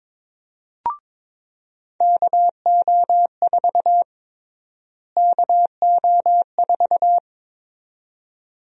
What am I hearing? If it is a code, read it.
Morse code, heard as a keyed tone, about 22 wpm, 700 Hz. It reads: KO4 KO4
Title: USA Call Sign Prefixes - Repeated 2x - 22wpm